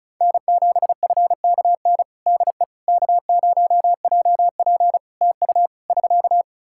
Morse code, heard as a keyed tone, about 36 wpm, 700 Hz. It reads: N7FKN DE K0JP TU <SK>
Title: Practice QSO - Exchange 4 - Ending - 35wpm